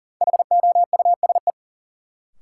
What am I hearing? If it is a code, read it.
Morse code, heard as a keyed tone, about 40 wpm, 700 Hz. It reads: HOUSE
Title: Sentences from Top 200 Words - New Word Drill - 40wpm